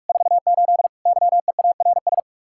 Morse code, heard as a keyed tone, about 50 wpm, 700 Hz. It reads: 48 YEARS